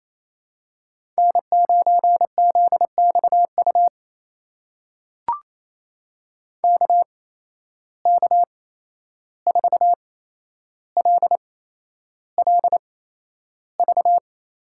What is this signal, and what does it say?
Morse code, heard as a keyed tone, about 28 wpm, spaced as for 11 wpm, 700 Hz. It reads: N9ZXU KK4LLV
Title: USA Call Signs - 8x Character Spacing - 28wpm